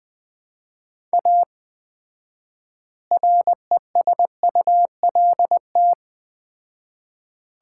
Morse code, heard as a keyed tone, about 21 wpm, 700 Hz. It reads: A RESULT
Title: 4x Word-Spacing - Sets of 3 Words - 20wpm